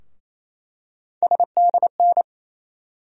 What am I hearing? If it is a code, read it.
Morse code, heard as a keyed tone, about 28 wpm, 700 Hz. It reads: SDN
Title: USA Call Sign Postfixes - 3 Letters 28wpm